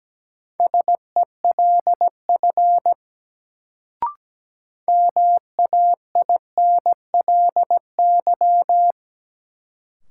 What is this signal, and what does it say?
Morse code, heard as a keyed tone, about 17 wpm, 700 Hz. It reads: SELF MAINLY
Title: Sentences from Top 1400 Words - New Word Drill Alt - 17wpm